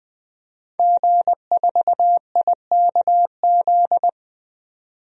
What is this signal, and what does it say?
Morse code, heard as a keyed tone, about 20 wpm, 700 Hz. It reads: G4IKZ
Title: DXCC-100 Call Signs - 5 Characters - 20wpm